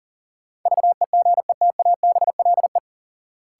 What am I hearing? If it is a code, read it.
Morse code, heard as a keyed tone, about 40 wpm, 700 Hz. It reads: VEGETABLE
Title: Sentences from Top 1300 Words - New Word Drill Alt - 40wpm